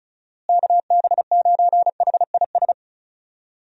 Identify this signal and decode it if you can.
Morse code, heard as a keyed tone, about 36 wpm, 700 Hz. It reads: KB9HIS
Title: USA Call Signs - 6 Characters - Repeated 2x - 35wpm